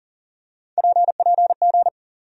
Morse code, heard as a keyed tone, about 40 wpm, 700 Hz. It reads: PPG
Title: USA Call Sign Postfixes - 3 Letters 40wpm